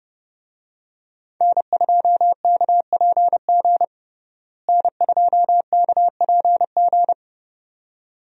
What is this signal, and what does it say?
Morse code, heard as a keyed tone, about 32 wpm, 700 Hz. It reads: N2KPG N2KPG
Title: USA Call Signs - 5 Characters - Repeated 2x - 30wpm